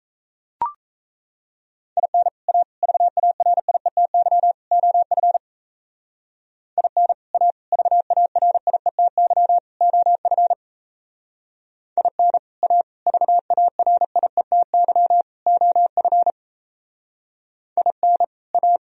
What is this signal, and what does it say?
Morse code, heard as a keyed tone, about 34 wpm, 700 Hz. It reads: IN A VARIETY OF IN A VARIETY OF IN A VARIETY OF IN A
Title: Speed-Racing - Sets of 4 Words 28wpm